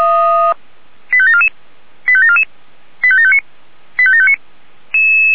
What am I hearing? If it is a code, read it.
Five-tone selective call (ZVEI), heard as quick runs of five tones, tones of about 80 ms. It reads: 85744 85744 85759 85759